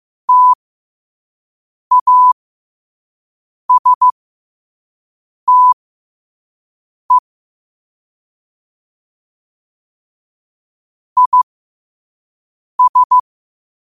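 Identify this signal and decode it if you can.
Morse code, heard as a keyed tone, about 15 wpm, spaced as for 5 wpm, 1000 Hz. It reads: TASTE IS
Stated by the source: Morse quote of the day at 5 wpm.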